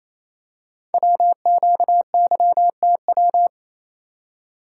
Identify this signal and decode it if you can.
Morse code, heard as a keyed tone, about 27 wpm, 700 Hz. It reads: WQYTW
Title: Code Groups - 5 Letters - 28wpm